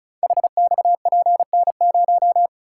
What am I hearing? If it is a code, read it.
Morse code, heard as a keyed tone, about 35 wpm, 700 Hz. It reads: HXPN0
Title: Code Groups - 5 Letter:Numbers - 35wpm